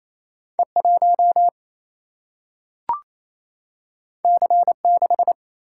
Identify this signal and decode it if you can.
Morse code, heard as a keyed tone, about 28 wpm, 700 Hz. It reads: E 1 C 6
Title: Single-Letter Single-Number 28wpm